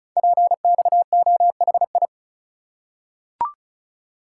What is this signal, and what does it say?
Morse code, heard as a keyed tone, about 36 wpm, 690 Hz. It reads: PXOHI
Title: Code Groups - 5 Letters - 35wpm